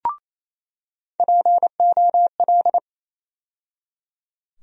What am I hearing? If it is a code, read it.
Morse code, heard as a keyed tone, about 28 wpm, 700 Hz. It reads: POL
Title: USA Call Sign Postfixes - 3 Letters 28wpm